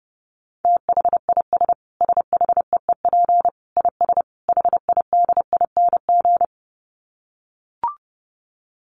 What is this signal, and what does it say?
Morse code, heard as a keyed tone, about 30 wpm, 700 Hz. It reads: THIS SHEEP IS HIDING
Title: Dit Challenge 30wpm